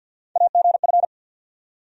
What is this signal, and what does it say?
Morse code, heard as a keyed tone, about 52 wpm, 700 Hz. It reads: AGF
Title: USA Call Sign Postfixes - 3 Letters 50wpm